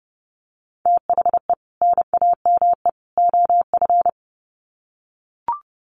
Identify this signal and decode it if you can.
Morse code, heard as a keyed tone, about 28 wpm, 700 Hz. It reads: THE NAME OF
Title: Sets of 3 Words 30wpm